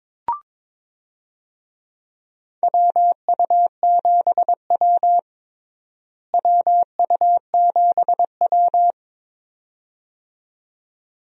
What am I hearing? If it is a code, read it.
Morse code, heard as a keyed tone, about 22 wpm, 700 Hz. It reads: WU7W WU7W
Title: USA Call Signs - 4 Characters - Repeated 2x - 22wpm